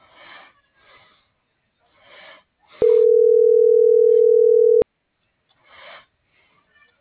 Ringback tone (North America), 440 + 480 Hz.